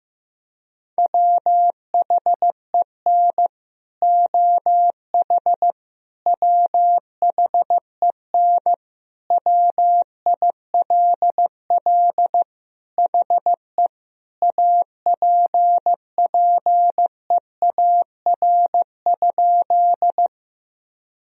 Morse code, heard as a keyed tone, about 15 wpm, 700 Hz. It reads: WHEN OH WHEN WILL HE APPEAR?
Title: Sentences from Top 1000 Words - Review Part 2 - 15wpm